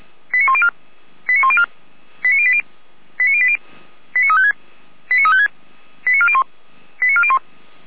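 Five-tone selective call (ZVEI), heard as quick runs of five tones, tones of about 70 ms. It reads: 89104 89104 89080 89080 89356 89356 89401 89401
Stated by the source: ZVEI five-tone call 89350, received by SDR.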